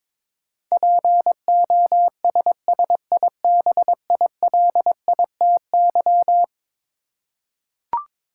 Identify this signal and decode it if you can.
Morse code, heard as a keyed tone, about 22 wpm, 700 Hz. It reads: POSSIBILITY
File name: practice / 22wpm / Eleven Letter Words - 22wpm